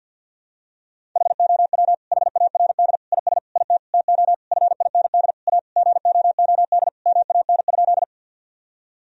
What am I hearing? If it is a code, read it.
Morse code, heard as a keyed tone, about 52 wpm, 700 Hz. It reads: HOW HARD IS IT TO FIND A GOOD MAN?
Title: Sentences from Top 1000 Words - Review Part 1 - 50wpm